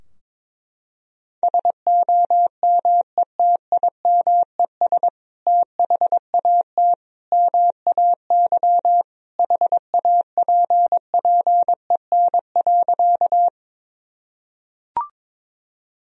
Morse code, heard as a keyed tone, about 22 wpm, 700 Hz. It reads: SOMETIMES THAT MAY HAPPEN.
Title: Sentences from Top 1000 Words - Review Part 3 - 22wpm